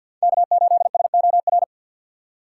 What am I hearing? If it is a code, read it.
Morse code, heard as a keyed tone, about 50 wpm, 700 Hz. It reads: K8SOR
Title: USA Call Signs - 5 Characters - 4x Character Spacing - 50wpm